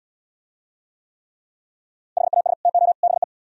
Morse code, heard as a keyed tone, about 76 wpm, 700 Hz. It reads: HAS TO BE